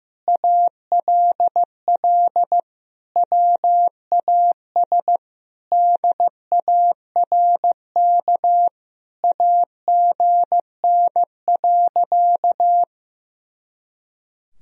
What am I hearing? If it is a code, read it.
Morse code, heard as a keyed tone, about 15 wpm, 700 Hz. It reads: ALL WAS DARK AGN.
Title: Sentences from Top 500 Words and Abbreviations 15wpm